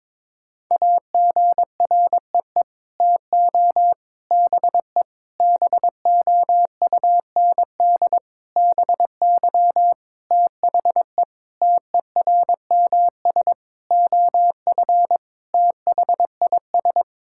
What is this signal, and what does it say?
Morse code, heard as a keyed tone, about 22 wpm, 700 Hz. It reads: AGREE TO BE BOUND BY THE TERMS OF THIS
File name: Sets of 9 Words - 22wpm